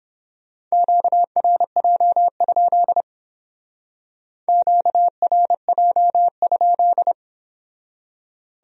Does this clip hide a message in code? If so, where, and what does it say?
Yes: Morse code from 0.7 to 7.1 s, heard as a keyed tone, about 30 wpm, 700 Hz. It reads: QRJ? QRJ?